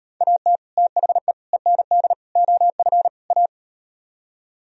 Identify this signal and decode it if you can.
Morse code, heard as a keyed tone, about 36 wpm, 700 Hz. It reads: AT THE END OF A